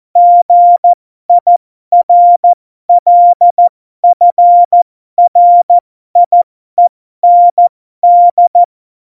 Morse code, heard as a keyed tone, about 14 wpm, spaced as for 12 wpm, 700 Hz. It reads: GIRLFRIEND
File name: Morsekurs / Witze